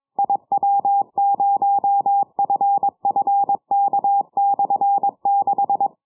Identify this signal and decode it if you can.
Morse code, heard as a keyed tone, about 22 wpm, 800 Hz. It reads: IW0FFK/6
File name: iw0ffk / 70cm / 2007 10 07 UR7D IW0FFK 6 980km